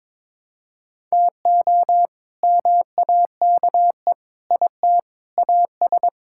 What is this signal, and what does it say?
Morse code, heard as a keyed tone, about 22 wpm, 700 Hz. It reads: TO MAKE IT AS
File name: practice / 22wpm / Sets of 8 Words - 22wpm